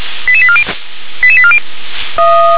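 Five-tone selective call (ZVEI), heard as quick runs of five tones, tones of about 70 ms. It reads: 88840 88840
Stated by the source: ZVEI five-tone call 88791 received by SDR.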